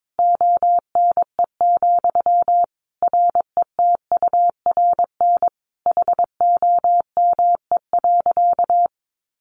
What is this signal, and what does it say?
Morse code, heard as a keyed tone, about 22 wpm, 700 Hz. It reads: ONE, RETURN HOME.
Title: Sentences from Top 800 Words 22wpm